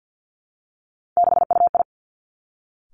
Morse code, heard as a keyed tone, about 50 wpm, 700 Hz. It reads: 6UI
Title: Three Challenging Letter-Number Combinations 50wpm